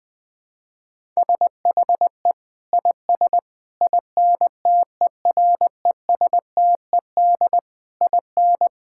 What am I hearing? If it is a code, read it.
Morse code, heard as a keyed tone, about 20 wpm, 700 Hz. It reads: SHE IS INTERESTED IN
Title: Sentences from Top 1000 Words - Review Part 6 - 20wpm